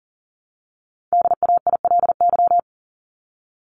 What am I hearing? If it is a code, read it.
Morse code, heard as a keyed tone, about 40 wpm, 700 Hz. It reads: DAILY
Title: Top 1001-1250 Words 40wpm